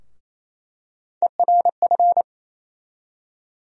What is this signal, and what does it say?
Morse code, heard as a keyed tone, about 28 wpm, 700 Hz. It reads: ERF
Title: USA Call Sign Postfixes - 3 Letters 28wpm